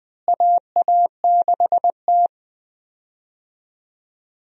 Morse code, heard as a keyed tone, about 20 wpm, 700 Hz. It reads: AA6T
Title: USA Call Signs - 4 Characters - Repeated 2x - 20wpm